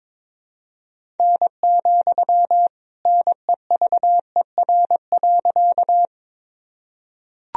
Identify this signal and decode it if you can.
Morse code, heard as a keyed tone, about 22 wpm, 700 Hz. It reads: N, NEVER.
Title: Sentences from Top 300 Words and Abbreviations 22wpm